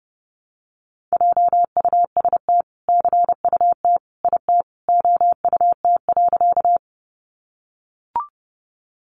Morse code, heard as a keyed tone, about 30 wpm, 700 Hz. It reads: JUST CUT IT OUT.
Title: Sentences from Top 300 Words 30wpm